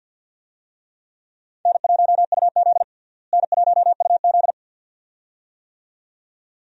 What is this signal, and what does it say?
Morse code, heard as a keyed tone, about 49 wpm, 700 Hz. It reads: N1UZ N1UZ
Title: USA Call Signs - 4 Characters - Repeated 2x - 50wpm